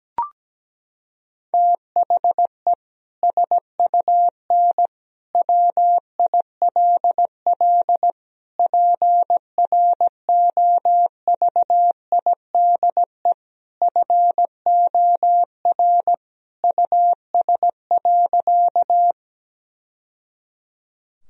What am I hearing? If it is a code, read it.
Morse code, heard as a keyed tone, about 17 wpm, 700 Hz. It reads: THE SUN WILL PROVIDE FOR US.
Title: Sentences from Top 1000 Words - Review Part 3 - 17wpm